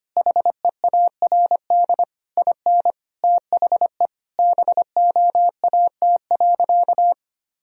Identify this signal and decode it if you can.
Morse code, heard as a keyed tone, about 25 wpm, 700 Hz. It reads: HEARD IN THE BOAT.
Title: Sentences from Top 1000 Words - Review Part 3 - 25wpm